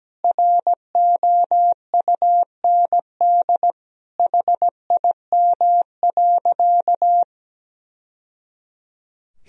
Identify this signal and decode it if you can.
Morse code, heard as a keyed tone, about 17 wpm, 700 Hz. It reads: ROUND HIM.
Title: Sentences from Top 1000 Words - Review Part 1 - 17wpm